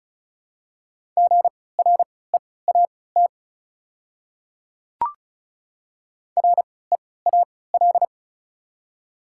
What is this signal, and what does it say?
Morse code, heard as a keyed tone, about 36 wpm, spaced as for 20 wpm, 690 Hz. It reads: GREAT REAL
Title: 3x Character Spacing - Top 500 Words - 35wpm